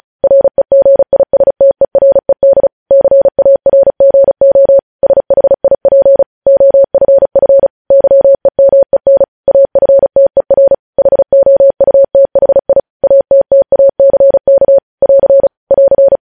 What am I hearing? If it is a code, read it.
Morse code, heard as a keyed tone, about 36 wpm, 550 Hz. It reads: REGISTERED CARGO SHIP OFF YEMEN AFTER HOUTHI ATTACK <AR> <AR>